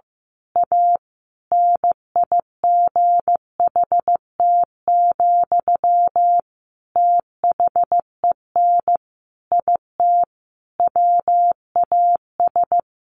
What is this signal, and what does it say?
Morse code, heard as a keyed tone, about 15 wpm, 700 Hz. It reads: A NIGHT, THEN IT WAS
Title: A Sound Of Thunder 15wpm